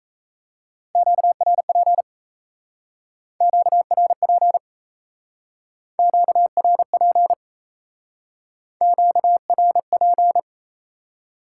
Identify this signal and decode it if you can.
Morse code, heard as a keyed tone, about 37 wpm, 700 Hz. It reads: QRP QRP QRP QRP